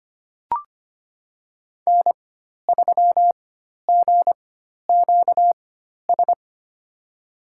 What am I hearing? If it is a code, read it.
Morse code, heard as a keyed tone, about 25 wpm, spaced as for 12 wpm, 700 Hz. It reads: N3GQS
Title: USA Call Signs - 5 Characters - 4x Character Spacing - 25wpm